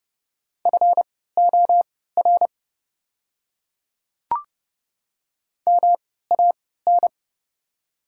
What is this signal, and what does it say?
Morse code, heard as a keyed tone, about 30 wpm, spaced as for 17 wpm, 700 Hz. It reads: FOR MAN
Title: 3x Character Spacing - Top 300 Words - 30wpm